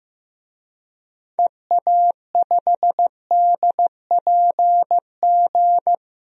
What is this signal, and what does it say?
Morse code, heard as a keyed tone, about 15 wpm, 700 Hz. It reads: EA5DPG
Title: DXCC-100 Call Signs - 6 Characters - 15wpm